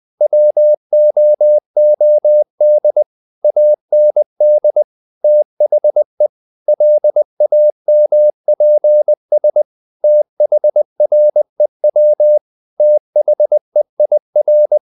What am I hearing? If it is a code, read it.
Morse code, heard as a keyed tone, about 20 wpm, 600 Hz. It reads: WOOD AND THE LAMPS THREW THEIR
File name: the old man and the sea part 2 20wpm